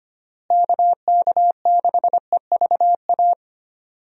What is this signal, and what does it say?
Morse code, heard as a keyed tone, about 25 wpm, 700 Hz. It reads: KK6EVA
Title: USA Call Signs - 6 Characters - Repeated 2x - 25wpm